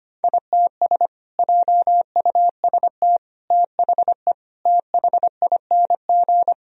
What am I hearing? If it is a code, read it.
Morse code, heard as a keyed tone, about 25 wpm, 700 Hz. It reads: ITS JUST THE THING
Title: Sentences from Top 1000 Words - Review Part 2 - 25wpm